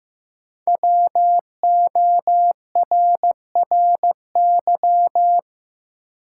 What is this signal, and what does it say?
Morse code, heard as a keyed tone, about 15 wpm, 700 Hz. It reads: WORRY
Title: Sentences from Top 1100 Words - New Word Drill - 15wpm